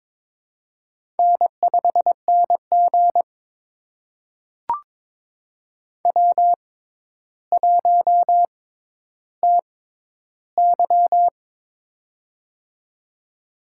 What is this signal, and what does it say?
Morse code, heard as a keyed tone, about 22 wpm, spaced as for 11 wpm, 700 Hz. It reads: N5NG W1TY
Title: USA Call Signs - 4 Characters - 6x Character Spacing - 22wpm